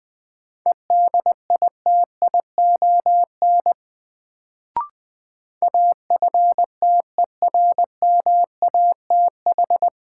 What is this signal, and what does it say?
Morse code, heard as a keyed tone, about 20 wpm, 700 Hz. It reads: EDITION AFTERMATH
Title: Sets of 5 Words - New Word Drill - 20wpm